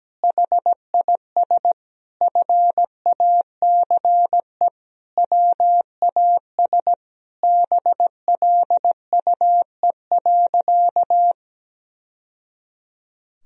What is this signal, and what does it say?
Morse code, heard as a keyed tone, about 17 wpm, 700 Hz. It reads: HIS FACE WAS BLUE.
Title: Sentences from Top 1000 Words - Review Part 3 - 17wpm